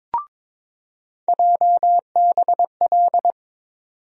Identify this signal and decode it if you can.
Morse code, heard as a keyed tone, about 22 wpm, 700 Hz. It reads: JBL
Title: USA Call Sign Postfixes - 3 Letters 22wpm